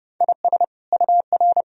Morse code, heard as a keyed tone, about 30 wpm, 700 Hz. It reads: IS UR
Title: Sentences from Top 400 Words and Abbreviations 30wpm